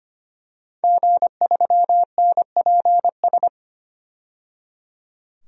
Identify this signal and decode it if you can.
Morse code, heard as a keyed tone, about 25 wpm, 700 Hz. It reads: G3NPS
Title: DXCC-100 Call Signs - 5 Characters - 25wpm